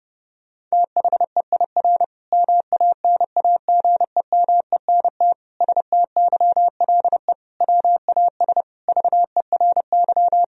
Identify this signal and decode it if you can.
Morse code, heard as a keyed tone, about 30 wpm, 700 Hz. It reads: THEIR MANAGEMENT STYLE WAS VERY
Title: Sentences from Top 1000 Words - Review Part 4 - 30wpm